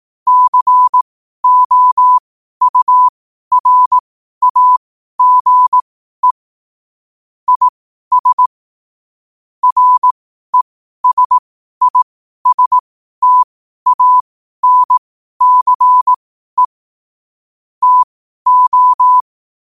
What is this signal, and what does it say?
Morse code, heard as a keyed tone, about 18 wpm, spaced as for 12 wpm, 1000 Hz. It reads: COURAGE IS RESISTANCE TO
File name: qotdfiles / 12wpm